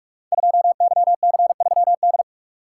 Morse code, heard as a keyed tone, about 44 wpm, 690 Hz. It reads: 2YC3D
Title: Code Groups - 5 Letter:Numbers - 45wpm